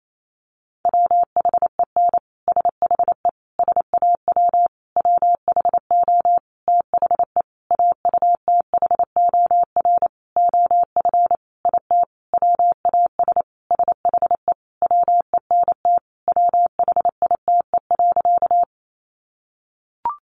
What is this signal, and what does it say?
Morse code, heard as a keyed tone, about 28 wpm, 700 Hz. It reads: WHEN SHE SAW WHO THE AUTHOR OF IT WAS SHE WENT WHITE.
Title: Sentences from Top 1200 Words 28wpm